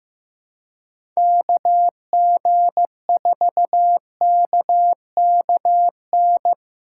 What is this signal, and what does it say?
Morse code, heard as a keyed tone, about 15 wpm, 700 Hz. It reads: KG4KKN
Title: USA Call Signs - 6 Characters - 4x Character Spacing - 15wpm